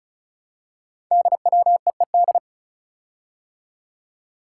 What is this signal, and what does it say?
Morse code, heard as a keyed tone, about 36 wpm, 700 Hz. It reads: DWEED
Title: Code Groups - 5 Letter:Numbers - 35wpm